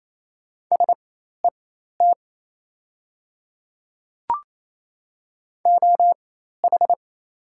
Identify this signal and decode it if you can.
Morse code, heard as a keyed tone, about 28 wpm, spaced as for 13 wpm, 700 Hz. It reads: SET OH
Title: 4x Character Spacing - Top 500 Words - 28wpm